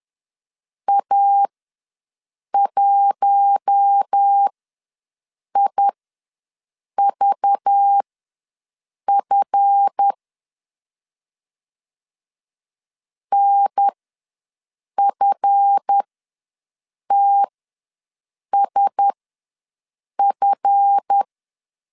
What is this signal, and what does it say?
Morse code, heard as a keyed tone, about 10 wpm, spaced as for 6 wpm, 800 Hz. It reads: A1IVF NFTSF
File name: LEZIONI / 7 GF9Z / 7 GF ws 10wpm 3918